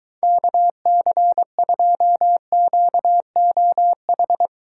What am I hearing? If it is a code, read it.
Morse code, heard as a keyed tone, about 23 wpm, 700 Hz. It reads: KC2QOH